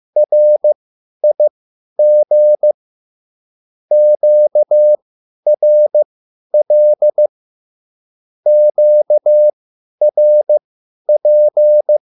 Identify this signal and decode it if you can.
Morse code, heard as a keyed tone, about 15 wpm, spaced as for 10 wpm, 600 Hz. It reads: RIG QRL QRP
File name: QSL / CW lesson